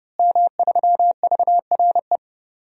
Morse code, heard as a keyed tone, about 32 wpm, 700 Hz. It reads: M3VRE
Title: DXCC-100 Call Signs - 5 Characters - Repeated 3x - 30wpm